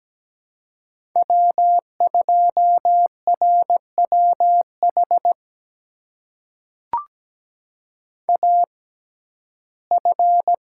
Morse code, heard as a keyed tone, about 17 wpm, spaced as for 12 wpm, 700 Hz. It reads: W2RWH AF
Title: USA Call Signs - 6x Character Spacing - 17wpm